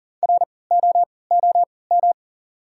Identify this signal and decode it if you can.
Morse code, heard as a keyed tone, about 40 wpm, spaced as for 23 wpm, 700 Hz. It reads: ROOM
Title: 3x Character Spacing - Top 300 Words - 40wpm